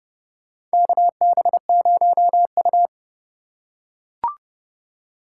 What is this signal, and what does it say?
Morse code, heard as a keyed tone, about 30 wpm, 700 Hz. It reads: KB0U
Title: USA Call Signs - 4 Characters - 2x Character Spacing - 30wpm